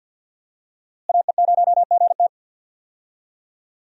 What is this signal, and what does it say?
Morse code, heard as a keyed tone, about 47 wpm, 700 Hz. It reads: AE0GT
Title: USA Call Signs - 5 Characters - Repeated 2x - 50wpm